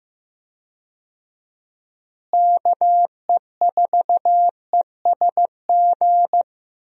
Morse code, heard as a keyed tone, about 15 wpm, 700 Hz. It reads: KE4ESG
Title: USA Call Signs - 6 Characters - Repeated 3x - 15wpm